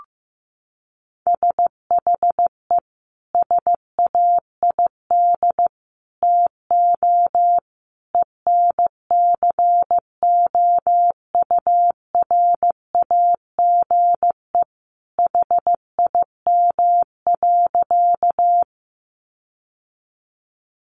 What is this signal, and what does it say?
Morse code, heard as a keyed tone, about 14 wpm, 700 Hz. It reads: SHE SAID TO ENCOURAGE HIM.
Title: Sentences from Top 1400 Words 15wpm